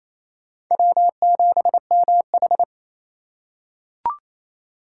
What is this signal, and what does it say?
Morse code, heard as a keyed tone, about 28 wpm, 700 Hz. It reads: W7MH
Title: USA Call Signs - 4 Characters - 3x Character Spacing - 28wpm